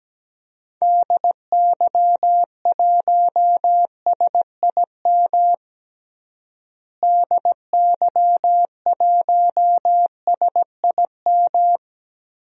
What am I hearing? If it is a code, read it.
Morse code, heard as a keyed tone, about 17 wpm, 700 Hz. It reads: DY1SIM DY1SIM
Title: DXCC-100 Call Signs - 6 Characters - Repeated 3x - 17wpm